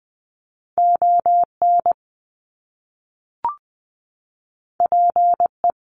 Morse code, heard as a keyed tone, about 20 wpm, 700 Hz. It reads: ON PE
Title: Canadian Provinces and Territories 20wpm